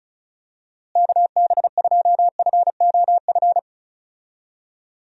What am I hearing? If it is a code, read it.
Morse code, heard as a keyed tone, about 36 wpm, 700 Hz. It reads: KB2FOF
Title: USA Call Signs - 6 Characters - Repeated 2x - 35wpm